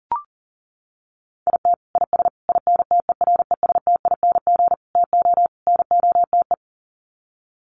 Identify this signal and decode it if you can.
Morse code, heard as a keyed tone, about 37 wpm, 700 Hz. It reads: IT IS INTERESTING TO NOTE
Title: Sets of 5 Words Encore2 40wpm